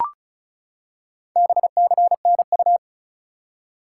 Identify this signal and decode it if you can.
Morse code, heard as a keyed tone, about 36 wpm, 700 Hz. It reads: BCNU
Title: Top 180 QSO Elements 35wpm